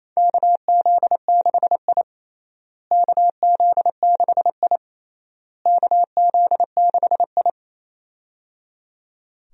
Morse code, heard as a keyed tone, about 28 wpm, 700 Hz. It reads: KZ6I KZ6I KZ6I
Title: USA Call Signs - 4 Characters - Repeated 3x - 28wpm